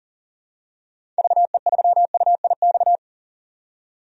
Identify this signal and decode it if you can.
Morse code, heard as a keyed tone, about 40 wpm, 700 Hz. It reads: VE3UIX
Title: DXCC-100 Call Signs - 6 Characters - 40wpm